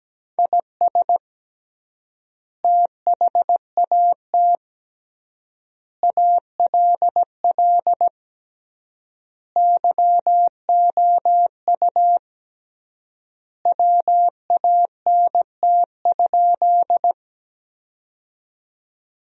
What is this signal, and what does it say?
Morse code, heard as a keyed tone, about 17 wpm, 700 Hz. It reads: IS THAT ALL YOU WANT?
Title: 3x Word-Spacing - Sentences from Top 200 Words - 17wpm